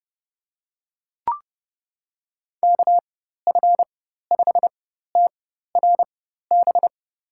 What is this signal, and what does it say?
Morse code, heard as a keyed tone, about 31 wpm, spaced as for 14 wpm, 700 Hz. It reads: KF5TRB
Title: USA Call Signs - 6 Characters - 4x Character Spacing - 30wpm